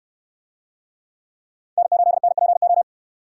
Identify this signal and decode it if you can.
Morse code, heard as a keyed tone, about 68 wpm, 690 Hz. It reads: N8NPO